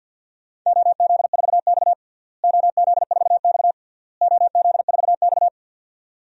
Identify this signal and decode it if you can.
Morse code, heard as a keyed tone, about 48 wpm, 700 Hz. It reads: OZ4X OZ4X OZ4X
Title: DXCC-100 Call Signs - 4 Characters - Repeated 3x - 50wpm